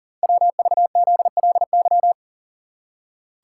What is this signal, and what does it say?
Morse code, heard as a keyed tone, about 40 wpm, 700 Hz. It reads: WVZLY
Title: Code Groups - 5 Letters - 40wpm